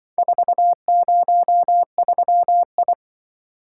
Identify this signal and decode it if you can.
Morse code, heard as a keyed tone, about 25 wpm, 700 Hz. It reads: 403I
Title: Speed-Racing - Four Letter-Number Combinations - 20wpm